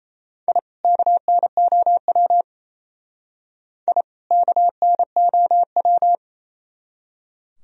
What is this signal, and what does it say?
Morse code, heard as a keyed tone, about 31 wpm, 700 Hz. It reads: I KNOW I KNOW
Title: Speed-Racing - Sets of 2 Words 28wpm